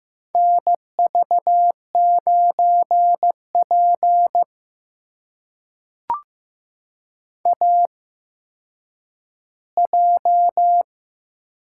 Morse code, heard as a keyed tone, about 15 wpm, spaced as for 9 wpm, 700 Hz. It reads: NV9P AJ
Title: USA Call Signs - 4 Characters - 8x Character Spacing - 15wpm